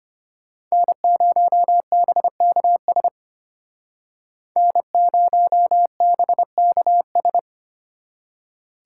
Morse code, heard as a keyed tone, about 32 wpm, 700 Hz. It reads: N0BKS N0BKS